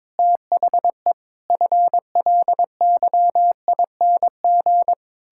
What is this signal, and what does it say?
Morse code, heard as a keyed tone, about 22 wpm, 700 Hz. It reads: THE FLYING
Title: Pangram Challenge v2 - 22wpm